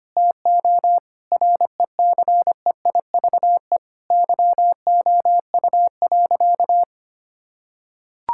Morse code, heard as a keyed tone, about 25 wpm, 700 Hz. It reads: TO RECEIVE YOU.
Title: Sentences from Top 1600 Words 25wpm